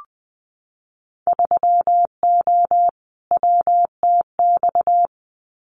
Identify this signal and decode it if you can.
Morse code, heard as a keyed tone, about 20 wpm, 700 Hz. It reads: 3O WTX
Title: Winter Field Day Exchanges Encore 20wpm